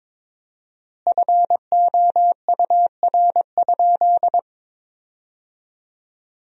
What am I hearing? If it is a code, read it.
Morse code, heard as a keyed tone, about 23 wpm, 700 Hz. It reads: FOUR?
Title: 4x Word-Spacing - Sentences from Top 300 Words - 22wpm